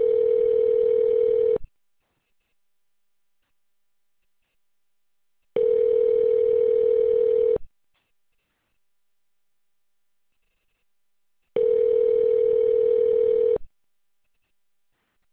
Ringback tone (North America), 440 + 480 Hz.